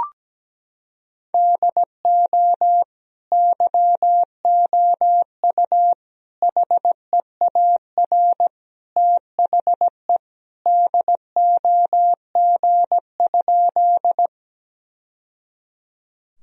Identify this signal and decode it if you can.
Morse code, heard as a keyed tone, about 17 wpm, 700 Hz. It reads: DO YOU HEAR THE DOG?
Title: Sentences from Top 1000 Words - Review Part 2 - 17wpm